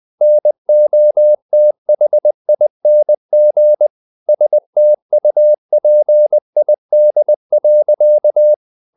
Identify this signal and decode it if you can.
Morse code, heard as a keyed tone, about 20 wpm, 600 Hz. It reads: NOTHING STUPID.
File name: the old man and the sea / the old man and the sea part 1 20wpm